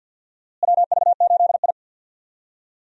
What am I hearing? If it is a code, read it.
Morse code, heard as a keyed tone, about 51 wpm, 700 Hz. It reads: WV8I
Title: USA Call Signs - 4 Characters - Repeated 3x - 50wpm